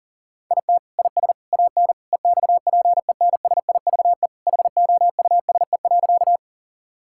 Morse code, heard as a keyed tone, about 40 wpm, 700 Hz. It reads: IT IS AN EXPENSIVE HOUSE.
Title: Sentences from Top 1000 Words - Review Part 6 - 40wpm